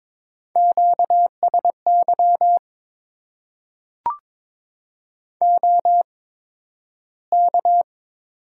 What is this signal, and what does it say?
Morse code, heard as a keyed tone, about 22 wpm, spaced as for 12 wpm, 700 Hz. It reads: QSY OK
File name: Top 90 QSO Elements - 8x Character Spacing - 22wpm